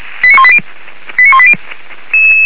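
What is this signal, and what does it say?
Five-tone selective call (ZVEI), heard as quick runs of five tones, tones of about 70 ms. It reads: 89180 89180
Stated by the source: Five-tone ZVEI selective call received on an SDR, from the feuerwehr-orlamuende.org (fire brigade Orlamünde) audio server.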